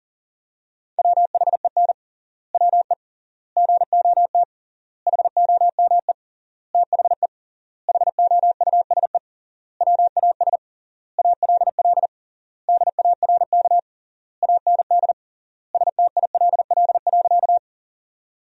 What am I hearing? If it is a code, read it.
Morse code, heard as a keyed tone, about 40 wpm, 700 Hz. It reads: WHEN WE GOT HOME THE HOUSE WAS ALL DARK AND STILL.
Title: 3x Word-Spacing - Sentences from Top 500 Words - 40wpm